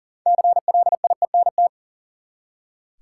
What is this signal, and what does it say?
Morse code, heard as a keyed tone, about 40 wpm, 700 Hz. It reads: CLIENT
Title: Six Letter Words - 40wpm